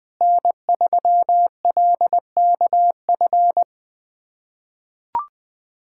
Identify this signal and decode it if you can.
Morse code, heard as a keyed tone, about 21 wpm, 700 Hz. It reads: N3LKF
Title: USA Callsign DE Callsign - 5 Characters - 20wpm